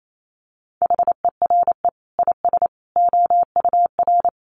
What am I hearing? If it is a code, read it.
Morse code, heard as a keyed tone, about 28 wpm, 700 Hz. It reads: HERE IS OUR
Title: Sentences from Top 500 Words 28wpm